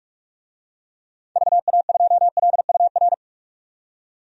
Morse code, heard as a keyed tone, about 41 wpm, 700 Hz. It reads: VA2LUR
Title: DXCC-100 Call Signs - 6 Characters - 45wpm